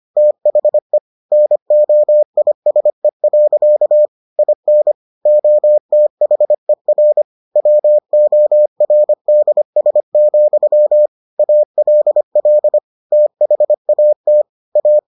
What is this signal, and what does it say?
Morse code, heard as a keyed tone, about 25 wpm, 600 Hz. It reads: THE NOISE. IN OTHER WORDS, ALL THAT A